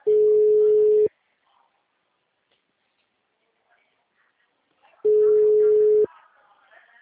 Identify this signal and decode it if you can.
Ringback tone (Europe, 425 Hz), with speech.
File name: out- -2012- - - .80683